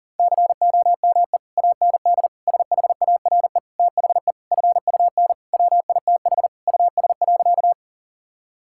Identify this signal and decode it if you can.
Morse code, heard as a keyed tone, about 37 wpm, 700 Hz. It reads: COME AND SHARE THE FUN WITH US.
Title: Sentences from Top 1000 Words - Review Part 6 - 40wpm